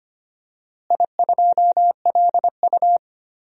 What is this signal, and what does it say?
Morse code, heard as a keyed tone, about 25 wpm, 700 Hz. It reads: I2LU
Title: DXCC-100 Call Signs - 4 Characters - Repeated 2x - 25wpm